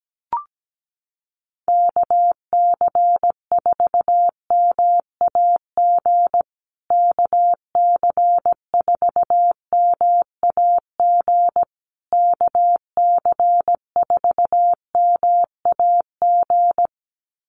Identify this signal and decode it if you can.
Morse code, heard as a keyed tone, about 17 wpm, 700 Hz. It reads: KC4MAG KC4MAG KC4MAG
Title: USA Call Signs Repeated 3x 17wpm